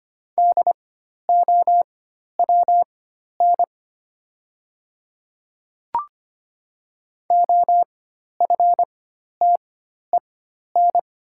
Morse code, heard as a keyed tone, about 25 wpm, spaced as for 12 wpm, 700 Hz. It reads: DOWN OFTEN
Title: 4x Character Spacing - Top 500 Words - 25wpm